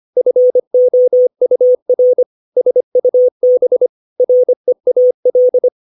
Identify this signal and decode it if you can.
Morse code, heard as a keyed tone, about 25 wpm, 500 Hz. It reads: FOUR SUB REAL
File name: 632 words / 632 Words 25WPM